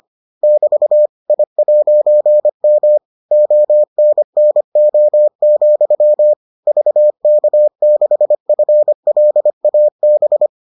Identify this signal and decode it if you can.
Morse code, heard as a keyed tone, about 25 wpm, 600 Hz. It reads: <BT> I'M ONNO, VK6FLAB